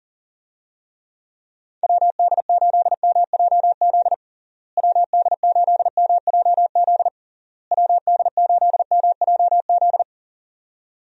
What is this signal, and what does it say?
Morse code, heard as a keyed tone, about 40 wpm, 700 Hz. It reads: WD8MJZ WD8MJZ WD8MJZ
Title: USA Call Signs - 6 Characters - Repeated 3x - 40wpm